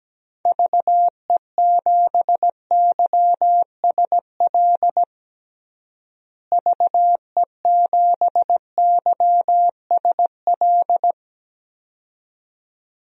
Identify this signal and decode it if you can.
Morse code, heard as a keyed tone, about 17 wpm, 700 Hz. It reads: VE7YSL VE7YSL
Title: DXCC-100 Call Signs - 6 Characters - Repeated 2x - 17wpm